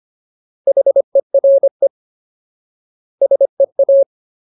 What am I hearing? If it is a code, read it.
Morse code, heard as a keyed tone, about 25 wpm, 550 Hz. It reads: HERE SEA
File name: Ultimate Word List / 25WPM Ultimate